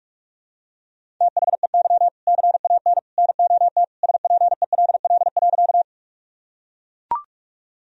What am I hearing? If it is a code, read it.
Morse code, heard as a keyed tone, about 43 wpm, 700 Hz. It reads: THEY CAN NOT SPELL.
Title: Sentences from Top 1000 Words - Review Part 1 - 45wpm